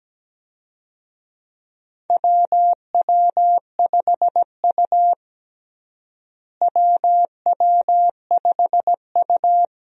Morse code, heard as a keyed tone, about 17 wpm, 700 Hz. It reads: WW5U WW5U
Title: USA Call Signs - 4 Characters - Repeated 2x - 17wpm